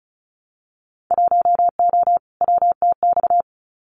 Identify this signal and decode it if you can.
Morse code, heard as a keyed tone, about 36 wpm, 700 Hz. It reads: 1O WTX
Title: Winter Field Day Exchanges Encore 35wpm